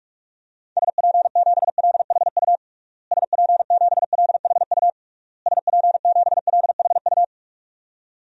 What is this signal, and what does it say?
Morse code, heard as a keyed tone, about 42 wpm, 700 Hz. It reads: SP7LHU SP7LHU SP7LHU
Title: DXCC-100 Call Signs - 6 Characters - Repeated 3x - 45wpm